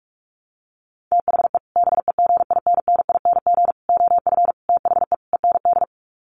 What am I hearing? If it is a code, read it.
Morse code, heard as a keyed tone, about 44 wpm, 700 Hz. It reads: THE BEGINNING OF THE END
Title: Sets of 5 Words Encore2 45wpm